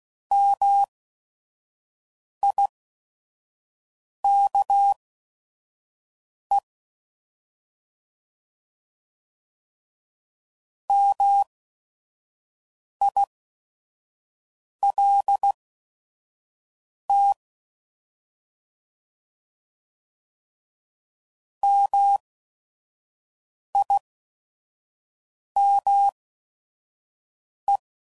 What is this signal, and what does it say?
Morse code, heard as a keyed tone, about 16 wpm, spaced as for 5 wpm, 800 Hz. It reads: MIKE MILT MIME